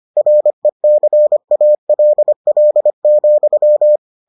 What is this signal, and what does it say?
Morse code, heard as a keyed tone, about 25 wpm, 600 Hz. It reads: RECALL,